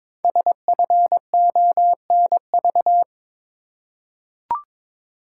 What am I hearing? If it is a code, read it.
Morse code, heard as a keyed tone, about 22 wpm, 700 Hz. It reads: SFONV
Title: Code Groups - 5 Letters - 22wpm